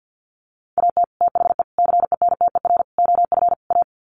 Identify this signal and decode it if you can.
Morse code, heard as a keyed tone, about 50 wpm, 700 Hz. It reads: AT THE CENTER OF A
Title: Sets of 5 Words Encore2 50wpm